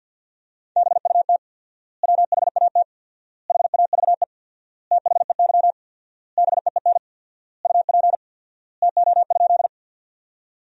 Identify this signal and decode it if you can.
Morse code, heard as a keyed tone, about 50 wpm, 700 Hz. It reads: BUT WHAT HAVE THEY BEEN UP TO?
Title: 4x Word-Spacing - Sentences from Top 100 Words - 50wpm